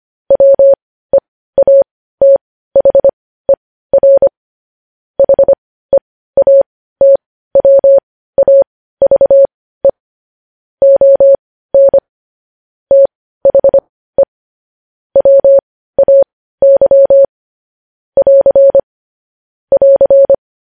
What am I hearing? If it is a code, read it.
Morse code, heard as a keyed tone, about 25 wpm, spaced as for 15 wpm, 550 Hz. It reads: WEATHER HEATWAVE ON THE WAY <AR> <AR>
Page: News Headlines in Morse Code at 15 WPM